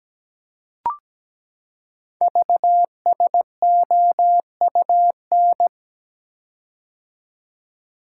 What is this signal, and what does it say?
Morse code, heard as a keyed tone, about 17 wpm, 700 Hz. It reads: VSOUN
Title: Code Groups - 5 Letters - 17wpm